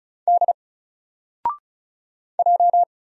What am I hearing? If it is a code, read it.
Morse code, heard as a keyed tone, about 36 wpm, 700 Hz. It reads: D J
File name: Single-Letter-Number-35wpm